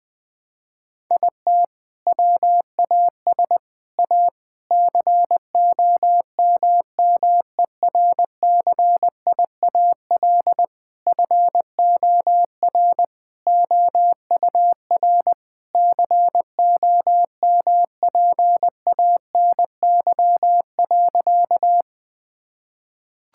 Morse code, heard as a keyed tone, about 21 wpm, 700 Hz. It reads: IT WAS A COMMERCIAL FOR OUR COMPANY.
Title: Sentences from Top 1000 Words - Review Part 5 - 20wpm